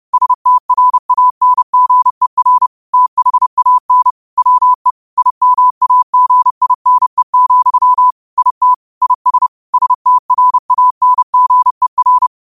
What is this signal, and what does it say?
Morse code, heard as a keyed tone, about 30 wpm, 1000 Hz. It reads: STRANGER THAN WE IMAGINE, IT IS STRANGER